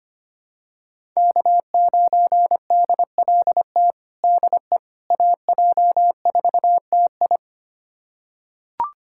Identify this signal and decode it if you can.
Morse code, heard as a keyed tone, about 25 wpm, 700 Hz. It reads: K9DLT DE AJ4TI
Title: USA Callsign DE Callsign - 5 Characters - 25wpm